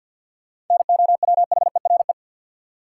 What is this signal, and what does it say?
Morse code, heard as a keyed tone, about 52 wpm, 700 Hz. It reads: NOWHERE
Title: Sets of 5 Words Encore - New Word Drill - 50wpm